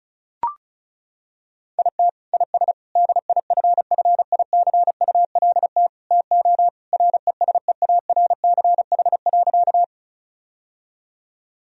Morse code, heard as a keyed tone, about 34 wpm, 700 Hz. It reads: IT IS DIFFICULT TO RESEARCH.
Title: Sentences from Top 1000 Words - Review Part 4 - 35wpm